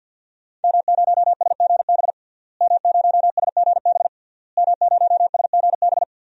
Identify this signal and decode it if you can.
Morse code, heard as a keyed tone, about 51 wpm, 700 Hz. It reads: M0SGB M0SGB M0SGB
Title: DXCC-100 Call Signs - 5 Characters - Repeated 3x - 50wpm